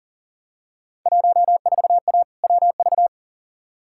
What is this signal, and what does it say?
Morse code, heard as a keyed tone, about 40 wpm, 700 Hz. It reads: 14A WV
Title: Field Day Exchanges Encore - 40wpm